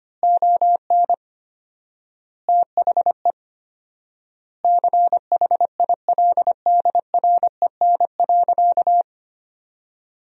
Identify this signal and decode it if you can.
Morse code, heard as a keyed tone, about 25 wpm, 700 Hz. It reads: ON THE CHILDREN.
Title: 4x Word-Spacing - Sentences from Top 300 Words - 25wpm